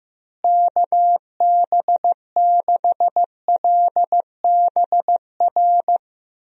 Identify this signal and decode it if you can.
Morse code, heard as a keyed tone, about 15 wpm, 700 Hz. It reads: KB6LBR
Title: USA Call Signs - 6 Characters - Repeated 2x - 15wpm